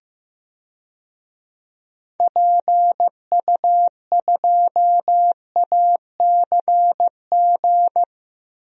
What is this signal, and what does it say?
Morse code, heard as a keyed tone, about 15 wpm, 700 Hz. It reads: PU2ACG
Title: DXCC-100 Call Signs - 6 Characters - Repeated 2x - 15wpm